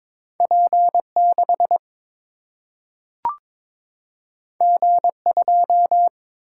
Morse code, heard as a keyed tone, about 22 wpm, 700 Hz. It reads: P 6 G 2
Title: Single-Letter Single-Number 22wpm